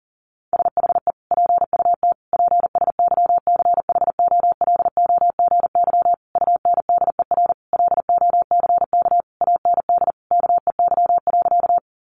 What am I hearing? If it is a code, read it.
Morse code, heard as a keyed tone, about 40 wpm, 700 Hz. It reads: SHE PUT PSYCHOLOGY UNDER LOCK AND KEY.
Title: Sentences from Top 1400 Words 40wpm